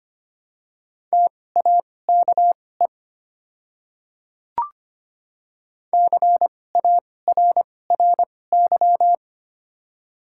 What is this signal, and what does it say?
Morse code, heard as a keyed tone, about 25 wpm, spaced as for 18 wpm, 700 Hz. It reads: TAKE CARRY
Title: 2x Character Spacing - Top 300 Words - 25wpm